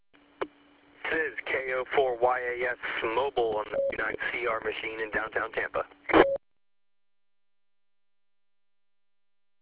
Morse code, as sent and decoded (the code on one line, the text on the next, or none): none